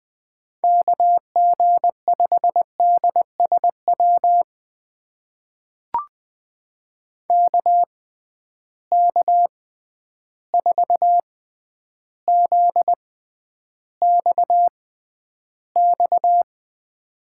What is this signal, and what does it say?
Morse code, heard as a keyed tone, about 20 wpm, spaced as for 10 wpm, 700 Hz. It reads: KG5DSW KK4ZXX
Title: USA Call Signs - 6 Characters - 6x Character Spacing - 20wpm